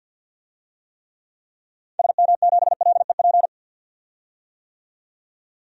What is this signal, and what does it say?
Morse code, heard as a keyed tone, about 52 wpm, 700 Hz. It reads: SM7LEP
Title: DXCC-100 Call Signs - 6 Characters - 50wpm